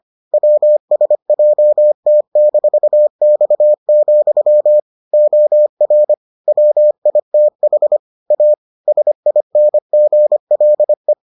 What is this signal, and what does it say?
Morse code, heard as a keyed tone, about 25 wpm, 600 Hz. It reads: WSJT-X, OR WITH A SINGLE